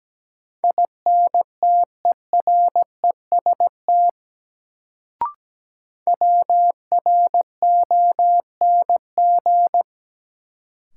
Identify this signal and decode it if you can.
Morse code, heard as a keyed tone, about 17 wpm, 700 Hz. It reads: INTEREST WRONG
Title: Sets of 5 Words Encore - New Word Drill - 17wpm